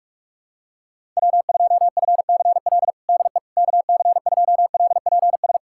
Morse code, heard as a keyed tone, about 45 wpm, 700 Hz. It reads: W2FCL DE KC2LPS
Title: USA Callsign DE Callsign 45wpm